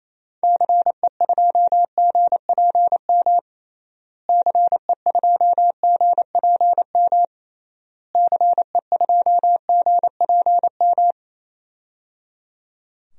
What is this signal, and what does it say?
Morse code, heard as a keyed tone, about 28 wpm, 700 Hz. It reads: CE2GPM CE2GPM CE2GPM
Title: DXCC-100 Call Signs - 6 Characters - Repeated 3x - 28wpm